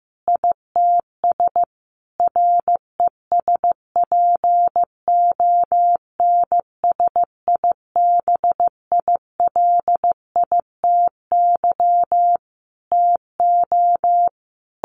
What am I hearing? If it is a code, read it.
Morse code, heard as a keyed tone, about 15 wpm, 700 Hz. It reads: ITS RESPONSIBILITY TO
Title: Sentences from Top 1200 Words 15wpm